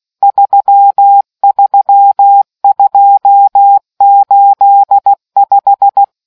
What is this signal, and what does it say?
Morse code, heard as a keyed tone, about 16 wpm, 800 Hz. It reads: 33285